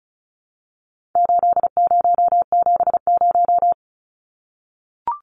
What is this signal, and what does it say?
Morse code, heard as a keyed tone, about 34 wpm, 700 Hz. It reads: 8070
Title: Numbers - 4-Digit 35wpm